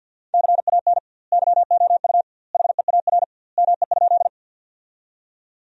Morse code, heard as a keyed tone, about 51 wpm, 700 Hz. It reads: CAN YOU HEAR ME?
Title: 2x Word-Spacing - Sentences from Top 300 Words - 50wpm